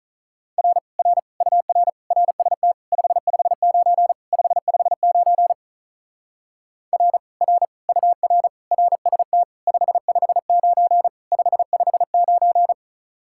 Morse code, heard as a keyed tone, about 46 wpm, 700 Hz. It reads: R R UR RST 559 559 R R UR RST 559 559